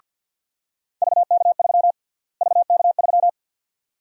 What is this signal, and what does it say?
Morse code, heard as a keyed tone, about 53 wpm, 700 Hz. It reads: VK3 VK3
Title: USA Call Sign Prefixes - Repeated 2x - 50wpm